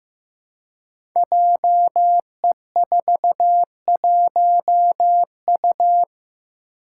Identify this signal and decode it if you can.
Morse code, heard as a keyed tone, about 15 wpm, 700 Hz. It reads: JE41U
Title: Five Challenging Letter-Number Combinations 15wpm